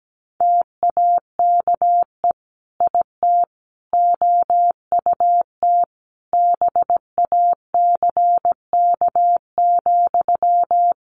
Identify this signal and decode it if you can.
Morse code, heard as a keyed tone, about 17 wpm, 700 Hz. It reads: TAKE IT OUT BACK,
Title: Sentences from Top 1500 Words 17wpm